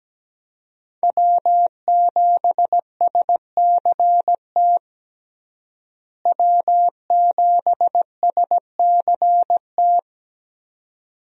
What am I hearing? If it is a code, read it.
Morse code, heard as a keyed tone, about 17 wpm, 700 Hz. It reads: W7SCT W7SCT
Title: USA Call Signs - 5 Characters - Repeated 2x - 17wpm